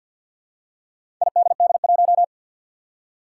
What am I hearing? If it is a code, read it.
Morse code, heard as a keyed tone, about 50 wpm, 700 Hz. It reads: IDD1
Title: Four Challenging Letter-Number Combinations 50wpm